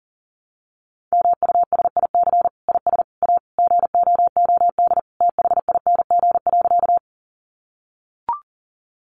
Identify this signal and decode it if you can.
Morse code, heard as a keyed tone, about 40 wpm, 700 Hz. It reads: MUSIC IS A GOOD THING.
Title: Sentences from Top 300 Words 40wpm